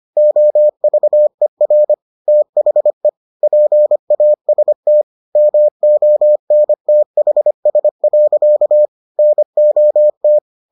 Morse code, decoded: OVER THE PAST MONTHS. NOT

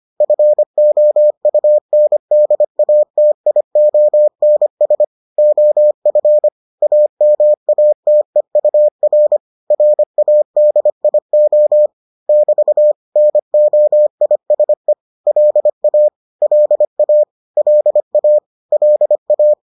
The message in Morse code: FOUNDATIONS OF AMATEUR RADIO <BT> NOISE LA LA LA LA

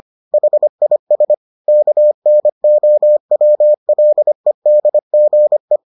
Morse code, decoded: HIS KNOWLEDGE

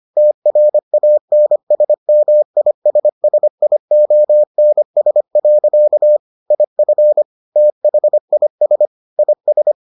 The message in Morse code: TRANSMISSIONS. IF THIS IS

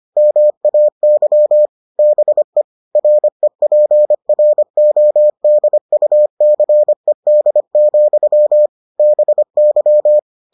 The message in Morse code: MAY BE REPRODUCED, BY

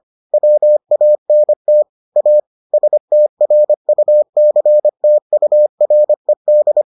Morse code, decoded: WANT A STRUCTURED

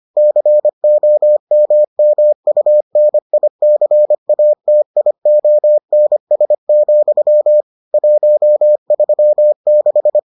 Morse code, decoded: COMMUNICATIONS, 136

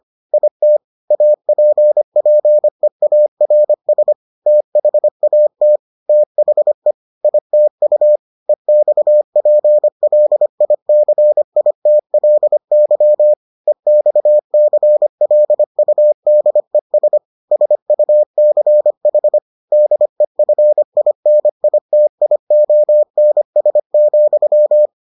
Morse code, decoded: IT APPEARS THAT THE ITU EXPLICITLY EXCLUDES SUCH DEFINITIONS,

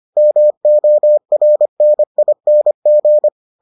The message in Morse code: MORNING

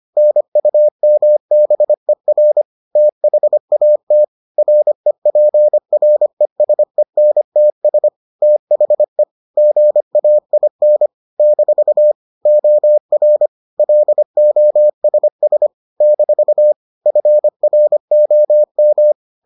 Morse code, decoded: NUMBER THAT REPRESENTS THE GAIN - OR LOSS - FROM

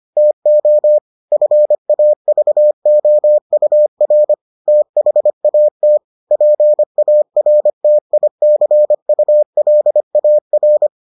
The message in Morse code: TO FAVOUR THAT PARTICULAR

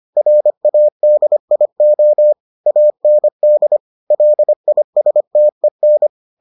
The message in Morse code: RADIO AND LISTEN

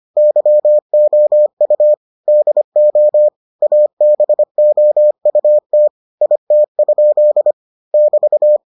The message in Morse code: YOU DO ABOUT IT? <BT>

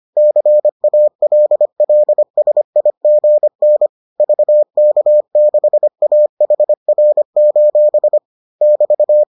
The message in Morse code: CALLSIGN VK6AHR: <BT>

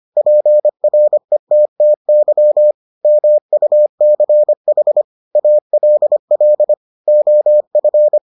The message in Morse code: PRETTY MUCH ALL OF

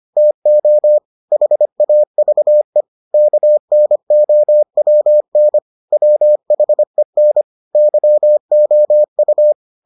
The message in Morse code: TO HAVE KNOWN WHEN YOU

